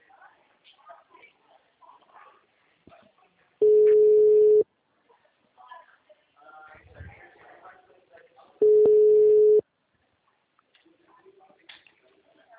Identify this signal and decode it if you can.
Ringback tone (Europe, 425 Hz), with speech.
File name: out- -2002- - - .187